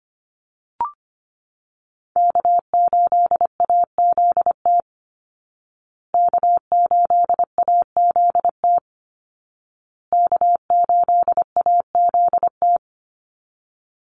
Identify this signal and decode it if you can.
Morse code, heard as a keyed tone, about 25 wpm, 700 Hz. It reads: K8AZT K8AZT K8AZT
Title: Call Signs Repeated 2x - 25wpm